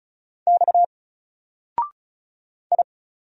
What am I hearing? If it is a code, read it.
Morse code, heard as a keyed tone, about 36 wpm, 700 Hz. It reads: X I
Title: Single Character (Letter:Number:Punctuation:Prosign) 35wpm